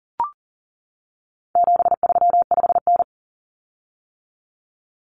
Morse code, heard as a keyed tone, about 40 wpm, 700 Hz. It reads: 735N
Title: Common Cut Numbers - 4-Digit 40wpm